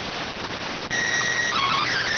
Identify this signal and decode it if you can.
SSTV (slow-scan television), PD 290.